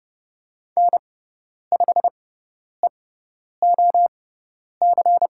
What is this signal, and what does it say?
Morse code, heard as a keyed tone, about 30 wpm, spaced as for 10 wpm, 700 Hz. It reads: N5EOC
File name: Lesson 18g - Letter m - Call Signs 30wpm